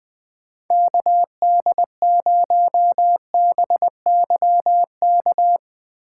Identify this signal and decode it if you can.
Morse code, heard as a keyed tone, about 20 wpm, 700 Hz. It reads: KD0BYK
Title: Speed Racing - USA 2x3 Call Signs 17wpm